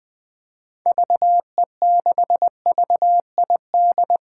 Morse code, heard as a keyed tone, about 21 wpm, 700 Hz. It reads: VE6VID
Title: 2023 SOTA Chaser Call Signs - 20wpm